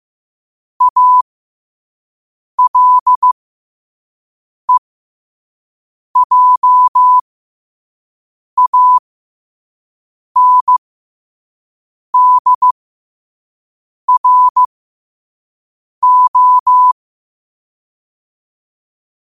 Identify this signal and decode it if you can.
Morse code, heard as a keyed tone, about 15 wpm, spaced as for 5 wpm, 1000 Hz. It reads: ALEJANDRO
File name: qotdfiles / 5wpm